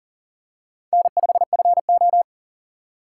Morse code, heard as a keyed tone, about 37 wpm, 700 Hz. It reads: N5FO
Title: USA Call Signs - 4x Character Spacing - 40wpm